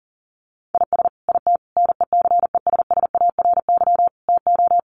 Morse code, heard as a keyed tone, about 40 wpm, 700 Hz. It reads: IS IT NECESSARY TO
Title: Sentences from Top 1500 Words 40wpm